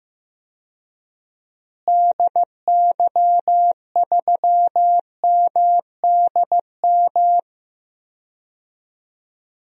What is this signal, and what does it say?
Morse code, heard as a keyed tone, about 16 wpm, 700 Hz. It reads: DY3MDM
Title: DXCC-100 Call Signs - 6 Characters - Repeated 2x - 15wpm